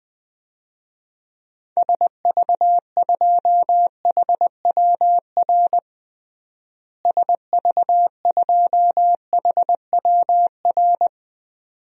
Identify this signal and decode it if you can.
Morse code, heard as a keyed tone, about 20 wpm, 700 Hz. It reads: SV2HWR SV2HWR
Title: DXCC-100 Call Signs - 6 Characters - Repeated 2x - 20wpm